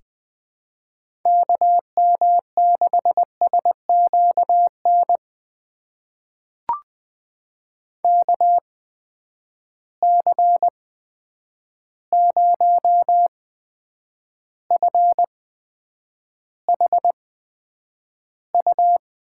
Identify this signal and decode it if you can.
Morse code, heard as a keyed tone, about 20 wpm, spaced as for 9 wpm, 700 Hz. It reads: KM6SQN KC0FHU
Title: USA Call Signs - 6 Characters - 8x Character Spacing - 20wpm